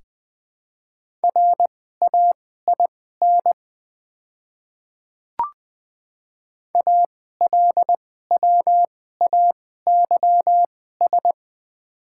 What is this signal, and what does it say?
Morse code, heard as a keyed tone, about 20 wpm, spaced as for 14 wpm, 700 Hz. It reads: RAIN ALWAYS
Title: 2x Character Spacing - Top 300 Words - 20wpm